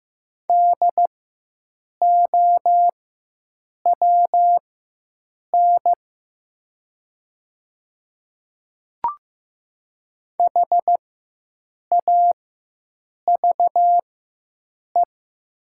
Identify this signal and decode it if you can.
Morse code, heard as a keyed tone, about 15 wpm, spaced as for 7 wpm, 700 Hz. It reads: DOWN HAVE
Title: 4x Character Spacing - Top 100 Words - 15wpm